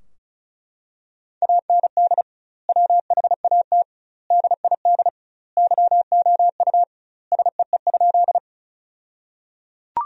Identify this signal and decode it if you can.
Morse code, heard as a keyed tone, about 36 wpm, 700 Hz. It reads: AND WHAT DID YOU SEE?
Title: 2x Word-Spacing - Sentences from Top 100 Words - 35wpm